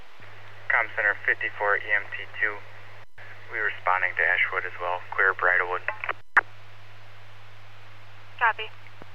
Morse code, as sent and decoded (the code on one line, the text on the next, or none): none